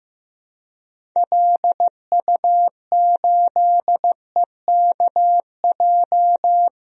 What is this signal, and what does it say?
Morse code, heard as a keyed tone, about 15 wpm, 700 Hz. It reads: LU8EKJ
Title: DXCC-100 Call Signs - 6 Characters - Repeated 3x - 15wpm